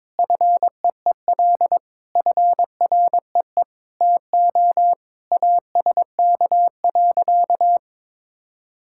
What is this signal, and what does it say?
Morse code, heard as a keyed tone, about 22 wpm, 700 Hz. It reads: FEEL FREE TO ASK.
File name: practice / 22wpm / Sentences with 8-Word Phrases - 22wpm